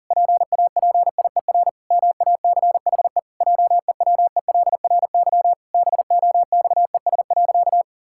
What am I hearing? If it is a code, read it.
Morse code, heard as a keyed tone, about 37 wpm, 700 Hz. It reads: PAPIER MACHE JEWELRY BOXES.
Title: Pangram Challenge v2 - 40wpm